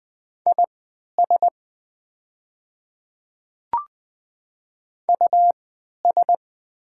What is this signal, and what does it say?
Morse code, heard as a keyed tone, about 20 wpm, spaced as for 11 wpm, 690 Hz. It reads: IS US
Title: 3x Character Spacing - Top 200 Words - 20wpm